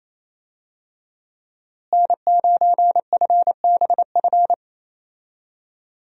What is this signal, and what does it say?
Morse code, heard as a keyed tone, about 27 wpm, 700 Hz. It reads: N9FBF
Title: USA Call Signs - 5 Characters - Repeated 2x - 28wpm